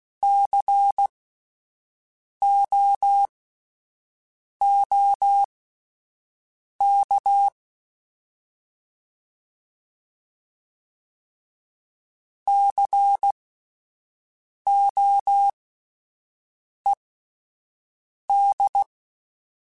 Morse code, heard as a keyed tone, about 16 wpm, spaced as for 5 wpm, 800 Hz. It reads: COOK COED
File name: Spaced Alphas / Spaced Alphas mp3s / Cs sp @10